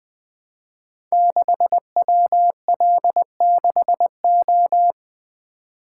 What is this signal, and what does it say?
Morse code, heard as a keyed tone, about 20 wpm, 700 Hz. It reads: 6WL6O
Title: Code Groups - 5 Letter:Numbers - 20wpm